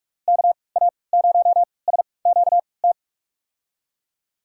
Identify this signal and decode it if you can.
Morse code, heard as a keyed tone, about 47 wpm, spaced as for 26 wpm, 700 Hz. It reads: KA0SQT
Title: USA Call Signs - 6 Characters - 3x Character Spacing - 45wpm